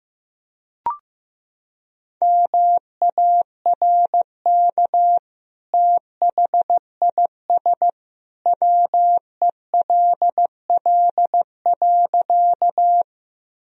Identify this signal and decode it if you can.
Morse code, heard as a keyed tone, about 15 wpm, 700 Hz. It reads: MARK THIS WELL.
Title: Sentences from Top 1000 Words - Review Part 1 - 15wpm